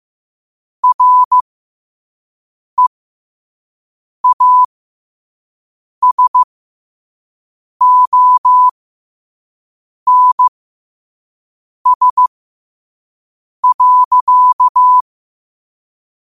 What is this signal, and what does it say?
Morse code, heard as a keyed tone, about 15 wpm, spaced as for 5 wpm, 1000 Hz. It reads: REASONS.
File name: qotdfiles / 5wpm